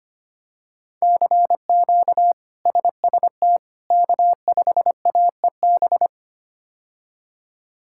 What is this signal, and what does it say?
Morse code, heard as a keyed tone, about 25 wpm, 700 Hz. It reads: CQ SST K5AEB
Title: K1USN SST QSO Practice - 25wpm